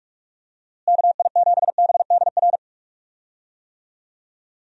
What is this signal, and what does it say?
Morse code, heard as a keyed tone, about 46 wpm, 700 Hz. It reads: KI7BDR
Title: USA Call Signs - 6 Characters - Repeated 3x - 45wpm